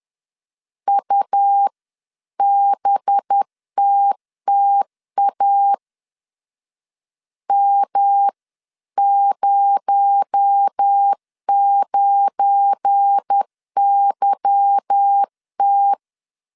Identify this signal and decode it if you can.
Morse code, heard as a keyed tone, about 10 wpm, 800 Hz. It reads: UBTTA M09YT
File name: CW / DET / 1 9 1500C S A 10wpm 4511